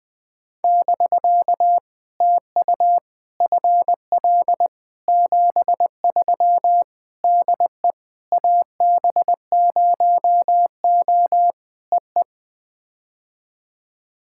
Morse code, heard as a keyed tone, about 19 wpm, 700 Hz. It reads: <BK> TU FL 73 DE AB0O EE
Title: POTA QSO Practice - 20wpm